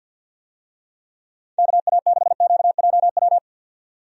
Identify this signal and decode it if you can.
Morse code, heard as a keyed tone, about 48 wpm, 700 Hz. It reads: KA6QJW
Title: USA Call Signs - 6 Characters - 50wpm